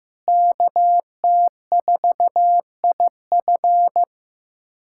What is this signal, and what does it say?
Morse code, heard as a keyed tone, about 15 wpm, 700 Hz. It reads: KT4IF
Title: USA Callsign DE Callsign - 5 Characters - 15wpm